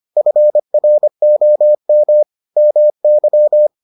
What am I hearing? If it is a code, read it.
Morse code, heard as a keyed tone, about 25 wpm, 600 Hz. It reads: FROM MY